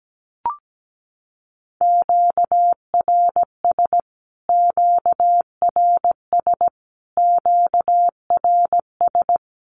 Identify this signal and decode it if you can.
Morse code, heard as a keyed tone, about 17 wpm, 700 Hz. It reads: QRS QRS QRS
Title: Q-Codes Repeated 3x 17wpm